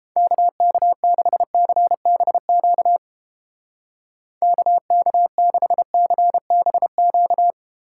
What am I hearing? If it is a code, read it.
Morse code, heard as a keyed tone, about 29 wpm, 700 Hz. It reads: KK6CBQ KK6CBQ